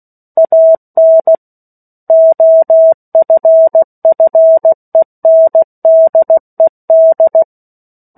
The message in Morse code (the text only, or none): AN OFFENDED